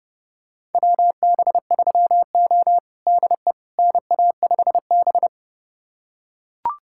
Morse code, heard as a keyed tone, about 30 wpm, 700 Hz. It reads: WB3O DE NA5B
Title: USA Callsign DE Callsign - 4 Characters - 30wpm